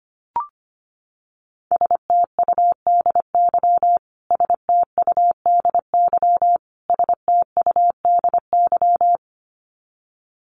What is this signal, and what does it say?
Morse code, heard as a keyed tone, about 25 wpm, 700 Hz. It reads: STUDY STUDY STUDY
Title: Top 300 Words Repeated 25wpm